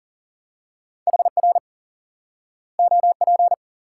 Morse code, heard as a keyed tone, about 40 wpm, 700 Hz. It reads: HR OP
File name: IQE Course 3 - Lesson 89 - YAGI - Two Elements - 40wpm